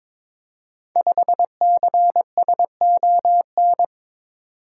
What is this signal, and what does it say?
Morse code, heard as a keyed tone, about 22 wpm, 700 Hz. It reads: 5CSON
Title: Code Groups - 5 Letter:Numbers - 22wpm